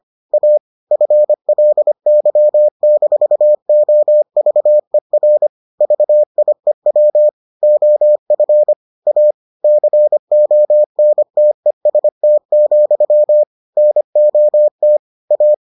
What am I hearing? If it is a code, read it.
Morse code, heard as a keyed tone, about 25 wpm, 600 Hz. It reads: A FLY-OVER VIEW OF A CONTEST, NOT A